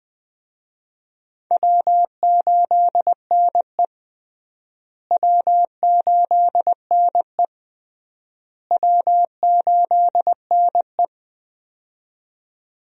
Morse code, heard as a keyed tone, about 20 wpm, 700 Hz. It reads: W8NE W8NE W8NE
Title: USA Call Signs - 4 Characters - Repeated 3x - 20wpm